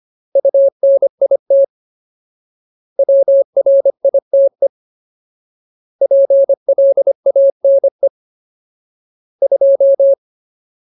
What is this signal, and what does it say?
Morse code, heard as a keyed tone, about 25 wpm, 550 Hz. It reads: UNIT WRITE PLANE 2
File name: Ultimate Word List / 25WPM Ultimate